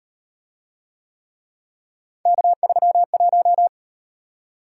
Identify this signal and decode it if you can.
Morse code, heard as a keyed tone, about 38 wpm, 700 Hz. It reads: K31